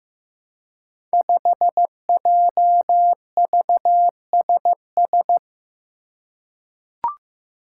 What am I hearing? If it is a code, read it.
Morse code, heard as a keyed tone, about 15 wpm, 700 Hz. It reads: 5JVSS
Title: Five Challenging Letter-Number Combinations 15wpm